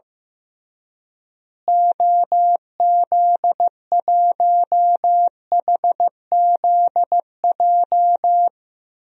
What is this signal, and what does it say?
Morse code, heard as a keyed tone, about 15 wpm, 700 Hz. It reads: OZ1HZJ
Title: DXCC-100 Call Signs - 6 Characters - Repeated 3x - 15wpm